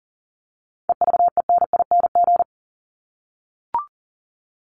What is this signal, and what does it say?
Morse code, heard as a keyed tone, about 40 wpm, 700 Hz. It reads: EVENING
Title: Top 1501-1750 Words 40wpm